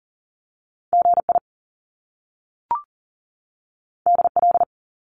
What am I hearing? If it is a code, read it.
Morse code, heard as a keyed tone, about 41 wpm, 700 Hz. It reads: GI DL
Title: Top Two Letter Combinations in Words 40wpm